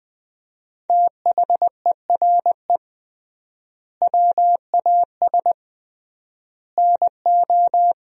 Morse code, heard as a keyed tone, about 20 wpm, 700 Hz. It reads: THERE WAS NO
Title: 3x Word-Spacing - Sentences from Top 300 Words - 20wpm